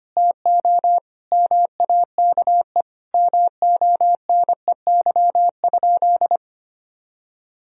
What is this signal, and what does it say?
Morse code, heard as a keyed tone, about 25 wpm, 700 Hz. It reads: TO MAKE MONEY?
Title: Sentences from Top 1000 Words - Review Part 4 - 25wpm